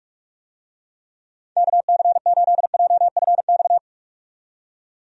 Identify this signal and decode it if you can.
Morse code, heard as a keyed tone, about 43 wpm, 700 Hz. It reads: KC8JFX
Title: USA Call Signs - 6 Characters - 45wpm